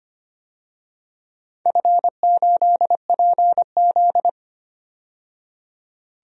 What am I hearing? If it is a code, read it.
Morse code, heard as a keyed tone, about 25 wpm, 700 Hz. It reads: F8PZ
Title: DXCC-100 Call Signs - 4 Characters - 25wpm